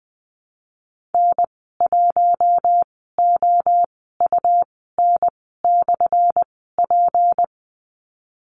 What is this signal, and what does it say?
Morse code, heard as a keyed tone, about 20 wpm, spaced as for 14 wpm, 700 Hz. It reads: N1OUN/P
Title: 179 Lesson 29g - Slash - Call Signs 20wpm